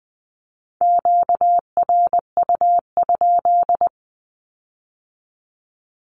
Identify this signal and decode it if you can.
Morse code, heard as a keyed tone, about 20 wpm, 700 Hz. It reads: QRU?
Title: Q-Codes with Spoken Meaning - 20wpm